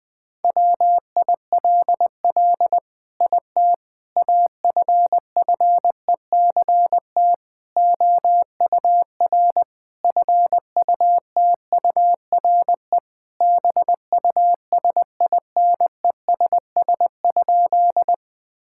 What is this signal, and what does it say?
Morse code, heard as a keyed tone, about 20 wpm, 700 Hz. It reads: WILL IT AFFECT OUR FUTURE BUSINESS?
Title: Sentences from Top 1000 Words - Review Part 6 - 20wpm